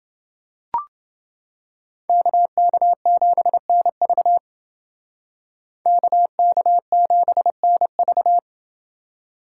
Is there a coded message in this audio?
Yes: Morse code from 2.1 to 8.4 s, heard as a keyed tone, about 29 wpm, 700 Hz. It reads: KK7NV KK7NV